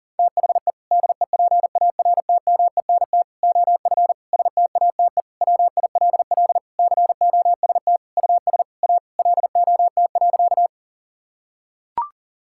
Morse code, heard as a keyed tone, about 40 wpm, 700 Hz. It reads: THE DEPARTMENT OF STATE WILL COST US A LOT.
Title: Sentences from Top 1000 Words - Review Part 5 - 40wpm